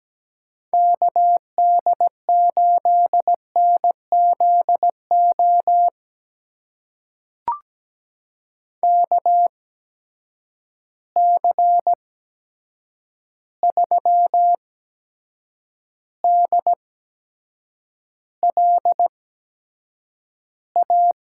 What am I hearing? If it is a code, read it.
Morse code, heard as a keyed tone, about 17 wpm, spaced as for 7 wpm, 700 Hz. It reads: KD8NZO KC3DLA
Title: USA Call Signs - 6 Characters - 8x Character Spacing - 17wpm